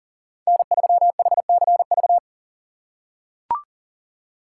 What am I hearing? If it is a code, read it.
Morse code, heard as a keyed tone, about 40 wpm, 700 Hz. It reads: N3HCV
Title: USA Call Signs - 5 Characters - 4x Character Spacing - 40wpm